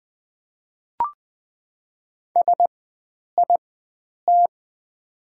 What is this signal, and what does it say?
Morse code, heard as a keyed tone, about 20 wpm, spaced as for 9 wpm, 700 Hz. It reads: SIT
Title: 4x Character Spacing - Top 500 Words - 20wpm